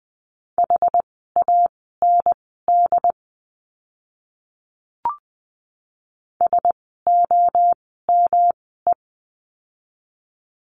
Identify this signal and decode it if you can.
Morse code, heard as a keyed tone, about 20 wpm, spaced as for 15 wpm, 700 Hz. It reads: HAND SOME
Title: 115 Lesson 19f - Letter w - Words 20wpm